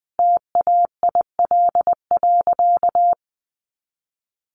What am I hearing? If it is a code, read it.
Morse code, heard as a keyed tone, about 20 wpm, 700 Hz. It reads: TAIL.
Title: Easy Phrases 20wpm